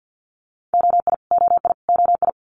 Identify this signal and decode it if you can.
Morse code, heard as a keyed tone, about 50 wpm, 690 Hz. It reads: OI OI OI
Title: Multi-letter Phonemes Repeated 3x 50wpm